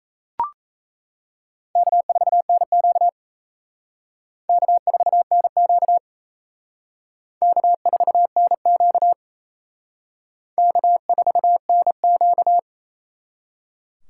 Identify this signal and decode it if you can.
Morse code, heard as a keyed tone, about 35 wpm, 700 Hz. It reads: K4NQ K4NQ K4NQ K4NQ
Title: Speed Racing - USA 1x2 Call Signs 28wpm